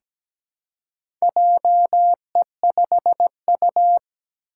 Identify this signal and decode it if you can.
Morse code, heard as a keyed tone, about 17 wpm, 700 Hz. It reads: JE5U
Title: Four Challenging Letter-Number Combinations 17wpm